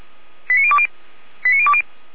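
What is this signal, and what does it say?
Five-tone selective call (ZVEI), heard as quick runs of five tones, tones of about 70 ms. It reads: 89020 89020
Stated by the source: ZVEI five-tone selective call 89027, received by SDR.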